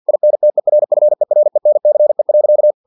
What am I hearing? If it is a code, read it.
Morse code, heard as a keyed tone, about 50 wpm, 600 Hz. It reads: INTERFERENCE.
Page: Never 49wpm